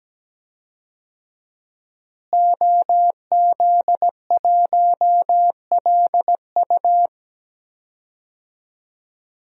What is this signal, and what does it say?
Morse code, heard as a keyed tone, about 17 wpm, 700 Hz. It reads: OZ1LU
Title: DXCC-100 Call Signs - 5 Characters - Repeated 2x - 17wpm